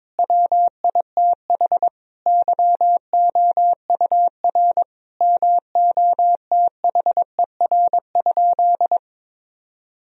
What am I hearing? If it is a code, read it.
Morse code, heard as a keyed tone, about 22 wpm, 700 Hz. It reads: WITH YOUR MOTHER?
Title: Sentences from Top 1000 Words - Review Part 3 - 22wpm